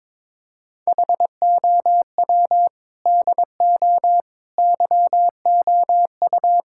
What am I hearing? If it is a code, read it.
Morse code, heard as a keyed tone, about 22 wpm, 700 Hz. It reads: HOW DO YOU
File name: practice / 22wpm / Sentences with 4-Word Phrases - 22wpm